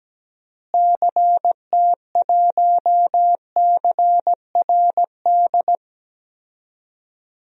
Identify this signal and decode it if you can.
Morse code, heard as a keyed tone, about 17 wpm, 700 Hz. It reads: CT1CRD
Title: DXCC-100 Call Signs - 6 Characters - Repeated 2x - 17wpm